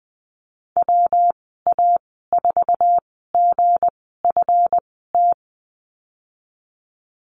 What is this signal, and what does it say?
Morse code, heard as a keyed tone, about 20 wpm, spaced as for 14 wpm, 700 Hz. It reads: WA4GFT
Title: 161 Lesson 26g - Letter g - Call Signs 20wpm